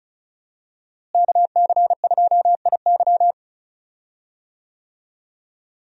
Morse code, heard as a keyed tone, about 36 wpm, 700 Hz. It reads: KC2IY
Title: USA Call Signs - 5 Characters - Repeated 3x - 35wpm